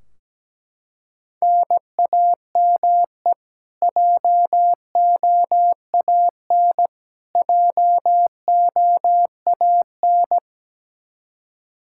Morse code, heard as a keyed tone, about 17 wpm, 700 Hz. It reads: NAME JOAN JOAN
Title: Practice QSO - Exchange 1 - Beginning - 17wpm